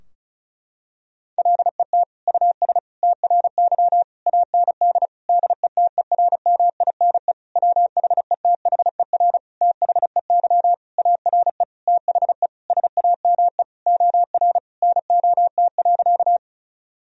Morse code, heard as a keyed tone, about 36 wpm, 700 Hz. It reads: LET US TRY AND DETERMINE WHETHER THEY ARE THE SAME OR NOT.
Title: Sentences from Top 1000 Words - Review Part 4 - 35wpm